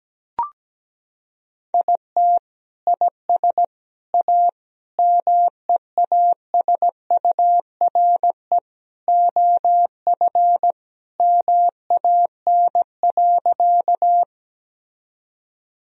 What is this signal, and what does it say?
Morse code, heard as a keyed tone, about 17 wpm, 700 Hz. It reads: IT IS A MEASURE OF MAN.
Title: Sentences from Top 1000 Words - Review Part 2 - 17wpm